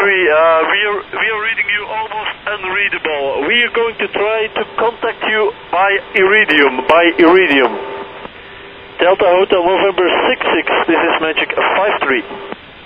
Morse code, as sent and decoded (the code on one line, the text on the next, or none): none